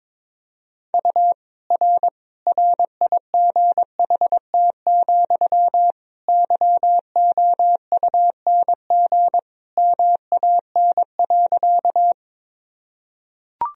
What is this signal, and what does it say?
Morse code, heard as a keyed tone, about 22 wpm, 700 Hz. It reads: U R RIGHT, YOUNG MAN.
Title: Sentences from Top 400 Words and Abbreviations 22wpm